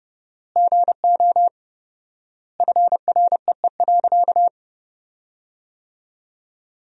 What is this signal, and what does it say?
Morse code, heard as a keyed tone, about 32 wpm, 700 Hz. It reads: GO FREE.
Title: 4x Word-Spacing - Sentences from Top 500 Words - 30wpm